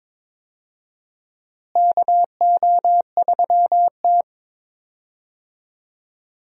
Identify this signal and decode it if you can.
Morse code, heard as a keyed tone, about 22 wpm, 700 Hz. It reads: KO3T
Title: USA Call Signs - 4 Characters - Repeated 2x - 22wpm